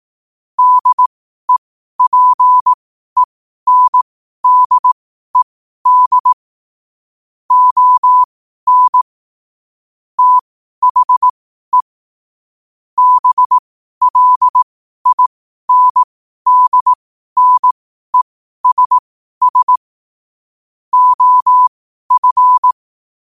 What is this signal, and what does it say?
Morse code, heard as a keyed tone, about 18 wpm, spaced as for 12 wpm, 1000 Hz. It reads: DEPENDED ON THE BLINDNESS OF
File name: qotdfiles / 12wpm